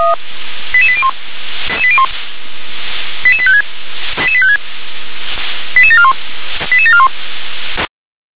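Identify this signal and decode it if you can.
Five-tone selective call (ZVEI), heard as quick runs of five tones, tones of about 70 ms. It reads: 88801 88856 88741 88741
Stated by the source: Received on an SDR as a ZVEI five-tone call, address 88770.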